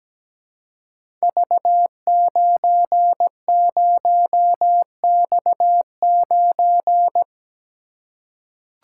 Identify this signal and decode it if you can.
Morse code, heard as a keyed tone, about 17 wpm, 700 Hz. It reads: V90X9
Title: Code Groups - 5 Letter:Numbers - 17wpm